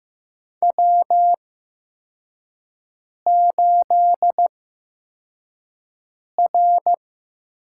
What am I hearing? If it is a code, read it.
Morse code, heard as a keyed tone, about 15 wpm, spaced as for 4 wpm, 700 Hz. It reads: W8R
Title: USA Call Signs - 8x Character Spacing - 15wpm